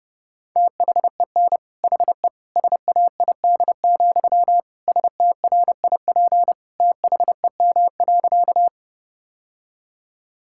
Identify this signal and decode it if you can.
Morse code, heard as a keyed tone, about 30 wpm, 700 Hz. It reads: THEN HE SAID, STRIP THEM.
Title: Sentences from Top 1600 Words 30wpm